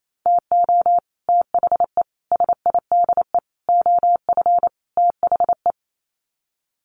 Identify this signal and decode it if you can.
Morse code, heard as a keyed tone, about 28 wpm, 700 Hz. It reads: TO THE SIDE OF THE
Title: Sets of 5 Words 28wpm